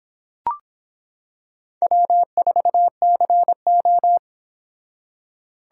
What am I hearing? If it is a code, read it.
Morse code, heard as a keyed tone, about 26 wpm, 700 Hz. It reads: W4CO